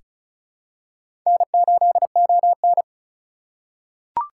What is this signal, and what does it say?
Morse code, heard as a keyed tone, about 36 wpm, 700 Hz. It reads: N8ON
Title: USA Call Signs - 4 Characters - 2x Character Spacing - 35wpm